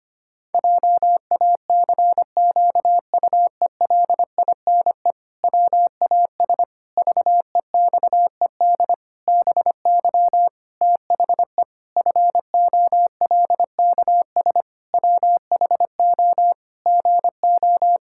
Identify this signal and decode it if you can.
Morse code, heard as a keyed tone, about 25 wpm, 700 Hz. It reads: JACQUELINE WAS VEXED BY THE FOLKS WHO GO
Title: Pangram Challenge v2 - 25wpm